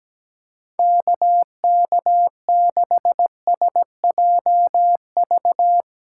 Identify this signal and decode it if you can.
Morse code, heard as a keyed tone, about 17 wpm, 700 Hz. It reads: KK6SJV
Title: USA Callsign DE Callsign - 6 Characters - 17wpm